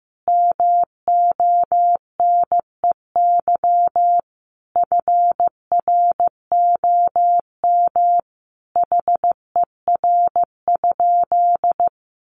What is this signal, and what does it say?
Morse code, heard as a keyed tone, about 15 wpm, 700 Hz. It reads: MONEY FROM HER?
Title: Sentences from Top 1500 Words 15wpm